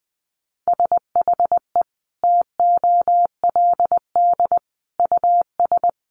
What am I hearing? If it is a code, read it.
Morse code, heard as a keyed tone, about 20 wpm, 700 Hz. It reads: SHE TOLD US
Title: Sentences from Top 1100 Words 20wpm